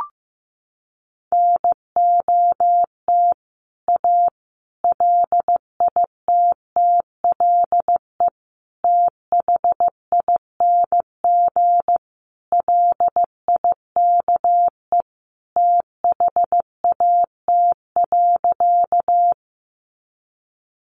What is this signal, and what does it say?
Morse code, heard as a keyed tone, about 15 wpm, 700 Hz. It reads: NOT A LITTLE THING LIKE THAT.
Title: A Sound Of Thunder 15wpm